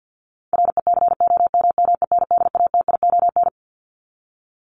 Morse code, heard as a keyed tone, about 50 wpm, 700 Hz. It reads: RECOMMENDATION
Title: Top 1251-1500 Words 50wpm